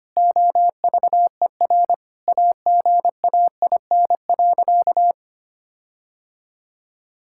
Morse code, heard as a keyed tone, about 25 wpm, 700 Hz. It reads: OVER AGAIN.
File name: practice / 25wpm / Sentences with 4-Word Phrases - 25wpm